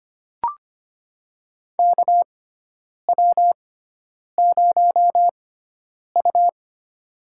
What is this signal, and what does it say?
Morse code, heard as a keyed tone, about 25 wpm, spaced as for 9 wpm, 700 Hz. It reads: KW0U
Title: USA Call Signs - 4 Characters - 6x Character Spacing - 25wpm